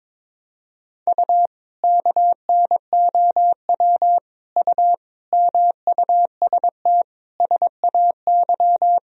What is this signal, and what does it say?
Morse code, heard as a keyed tone, about 21 wpm, 700 Hz. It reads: U KNOW U MUST SAY
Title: Sentences from Top 500 Words and Abbreviations 22wpm